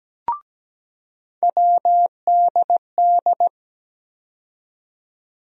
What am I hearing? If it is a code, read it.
Morse code, heard as a keyed tone, about 17 wpm, 700 Hz. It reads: WDD
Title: USA Call Sign Postfixes - 3 Letters 17wpm